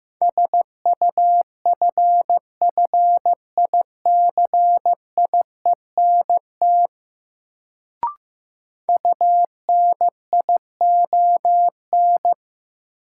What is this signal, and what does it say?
Morse code, heard as a keyed tone, about 15 wpm, 700 Hz. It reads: SUFFICIENT UNION
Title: Sentences from Top 1400 Words - New Word Drill Alt - 15wpm